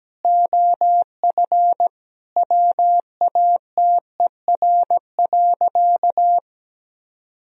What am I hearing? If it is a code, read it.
Morse code, heard as a keyed tone, about 17 wpm, 700 Hz. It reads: OF WATER.
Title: Sentences from Top 1000 Words - Review Part 1 - 17wpm